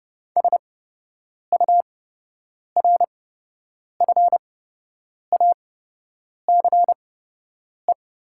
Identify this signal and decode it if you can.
Morse code, heard as a keyed tone, about 29 wpm, spaced as for 8 wpm, 700 Hz. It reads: SURFACE
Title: 8x Character Spacing - Top 500 Words - 30wpm